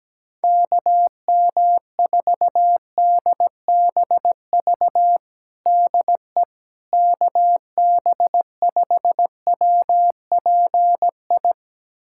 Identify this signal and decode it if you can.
Morse code, heard as a keyed tone, about 17 wpm, 700 Hz. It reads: KM4DBV DE KB5WPI
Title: USA Callsign DE Callsign - 6 Characters - 17wpm